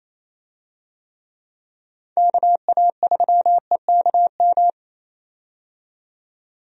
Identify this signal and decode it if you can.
Morse code, heard as a keyed tone, about 28 wpm, 700 Hz. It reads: KA3EKM
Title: USA Call Signs - 6 Characters - 28wpm